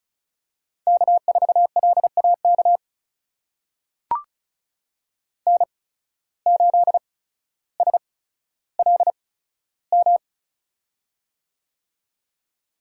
Morse code, heard as a keyed tone, about 36 wpm, spaced as for 15 wpm, 700 Hz. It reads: K4LAK N8SLM
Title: USA Call Signs - 5 Characters - 8x Character Spacing - 35wpm